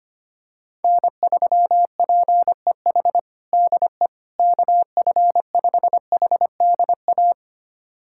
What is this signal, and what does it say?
Morse code, heard as a keyed tone, about 25 wpm, 700 Hz. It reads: N3PEH DE KF5HDA
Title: USA Callsign DE Callsign 25wpm